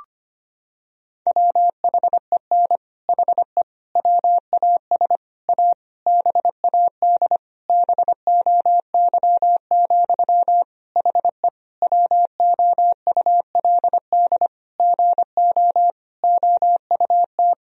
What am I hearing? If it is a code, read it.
Morse code, heard as a keyed tone, about 25 wpm, 700 Hz. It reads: WHEN HE WAS A BAD BOY, HE WOULD GO OUT
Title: Sentences from Top 1000 Words - Review Part 3 - 25wpm